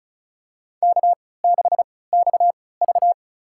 Morse code, heard as a keyed tone, about 36 wpm, spaced as for 20 wpm, 700 Hz. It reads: K6XV
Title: USA Call Signs - 4 Characters - 3x Character Spacing - 35wpm